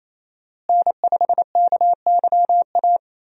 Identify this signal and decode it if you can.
Morse code, heard as a keyed tone, about 28 wpm, 700 Hz. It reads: N5KYA
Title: USA Call Signs - 5 Characters - Repeated 2x - 28wpm